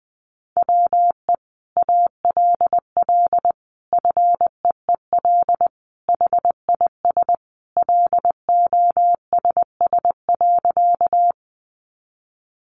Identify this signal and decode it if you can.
Morse code, heard as a keyed tone, about 19 wpm, 700 Hz. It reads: WE ALL FEEL HIS LOSS.
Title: Sentences from Top 1400 Words 20wpm